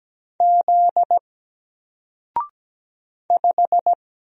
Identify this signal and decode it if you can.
Morse code, heard as a keyed tone, about 17 wpm, 700 Hz. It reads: Z 5
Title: Single Character (Letter:Number:Punctuation:Prosign) 17wpm